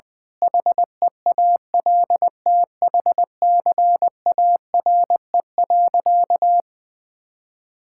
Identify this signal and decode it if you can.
Morse code, heard as a keyed tone, about 20 wpm, 700 Hz. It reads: HEALTHCARE.
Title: Sentences with 7-Word Phrases - 20wpm